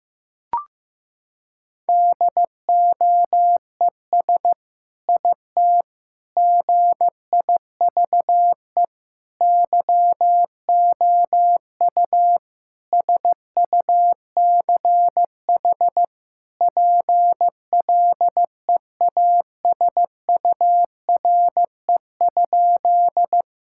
Morse code, heard as a keyed tone, about 15 wpm, 700 Hz. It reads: DOES IT GIVE YOU SUCH PLEASURE?
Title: Sentences from Top 1600 Words 15wpm